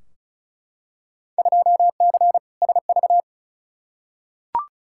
Morse code, heard as a keyed tone, about 36 wpm, 700 Hz. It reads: 2C SV
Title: Field Day Exchanges Encore - 35wpm